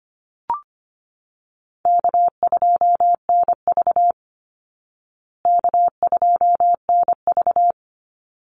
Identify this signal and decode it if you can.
Morse code, heard as a keyed tone, about 25 wpm, 700 Hz. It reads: K2NV K2NV
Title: Easy Call Signs Repeated 2x - 25wpm